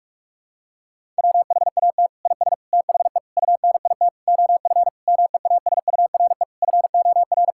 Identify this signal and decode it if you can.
Morse code, heard as a keyed tone, about 41 wpm, 700 Hz. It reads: WHAT IS THE UNIT OF MEASURE FOR
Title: Sentences from Top 1000 Words - Review Part 6 - 45wpm